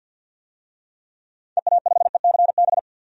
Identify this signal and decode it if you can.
Morse code, heard as a keyed tone, about 52 wpm, 700 Hz. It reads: EA5ECB
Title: DXCC-100 Call Signs - 6 Characters - 50wpm